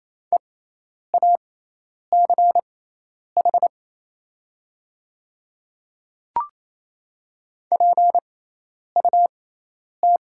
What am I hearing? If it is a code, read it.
Morse code, heard as a keyed tone, about 28 wpm, spaced as for 10 wpm, 700 Hz. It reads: EACH PUT
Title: 6x Character Spacing - Top 200 Words - 28wpm